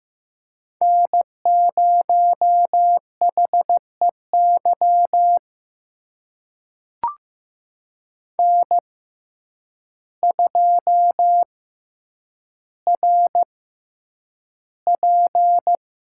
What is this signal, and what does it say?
Morse code, heard as a keyed tone, about 15 wpm, spaced as for 8 wpm, 700 Hz. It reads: N0HEY N2RP
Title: USA Call Signs - 5 Characters - 6x Character Spacing - 15wpm